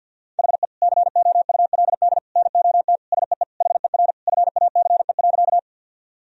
Morse code, heard as a keyed tone, about 50 wpm, 700 Hz. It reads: HE COULD NOT SEE HER FACE.
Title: Sentences from Top 1000 Words - Review Part 2 - 50wpm